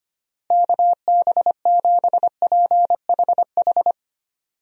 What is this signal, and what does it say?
Morse code, heard as a keyed tone, about 25 wpm, 700 Hz. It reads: KB7PHH
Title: USA Call Signs - 6 Characters - Repeated 3x - 25wpm